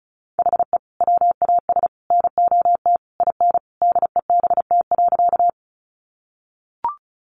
Morse code, heard as a keyed tone, about 34 wpm, 700 Hz. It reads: HE WAS NOT IN DEBT.
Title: Sentences from Top 1300 Words 35wpm